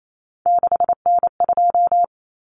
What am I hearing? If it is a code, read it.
Morse code, heard as a keyed tone, about 28 wpm, 700 Hz. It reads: 6N2
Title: Common Cut Numbers - 3-Digit 28wpm